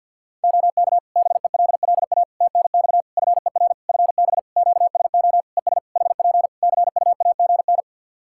Morse code, heard as a keyed tone, about 52 wpm, 700 Hz. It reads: OK BELLA TNX FER FB QSO ES HP CUAGN
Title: K1USN SST QSO Practice 50wpm